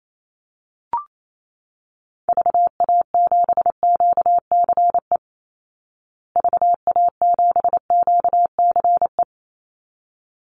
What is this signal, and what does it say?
Morse code, heard as a keyed tone, about 28 wpm, 700 Hz. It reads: VA7QCE VA7QCE
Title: Call Signs Repeated 3x - 28wpm